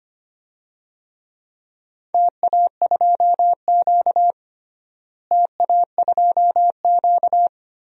Morse code, heard as a keyed tone, about 25 wpm, 700 Hz. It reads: TA2Q TA2Q
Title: DXCC-100 Call Signs - 4 Characters - Repeated 2x - 25wpm